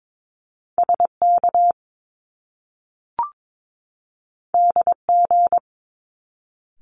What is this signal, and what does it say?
Morse code, heard as a keyed tone, about 22 wpm, 700 Hz. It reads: SK DG
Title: Top Two Letter Combinations in Words 22wpm